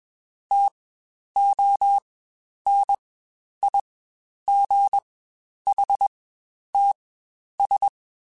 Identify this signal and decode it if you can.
Morse code, heard as a keyed tone, about 21 wpm, spaced as for 10 wpm, 800 Hz. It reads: TONIGHTS
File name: Spaced Characters / Spaced Characters-mp3s / Weather 3 17 sp @20